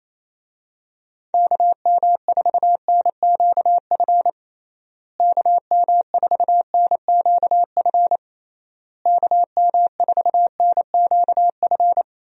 Morse code, heard as a keyed tone, about 28 wpm, 700 Hz. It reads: KM4NQF KM4NQF KM4NQF
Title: USA Call Signs - 6 Characters - Repeated 3x - 28wpm